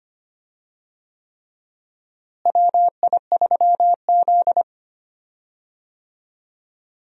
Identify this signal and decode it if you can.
Morse code, heard as a keyed tone, about 25 wpm, 700 Hz. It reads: WI3Z
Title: USA Call Signs - 4 Characters - Repeated 3x - 25wpm